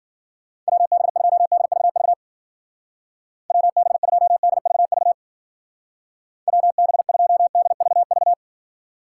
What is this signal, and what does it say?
Morse code, heard as a keyed tone, about 48 wpm, 700 Hz. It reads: WB2DVV WB2DVV WB2DVV
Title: Speed Racing - USA 2x3 Call Signs 40wpm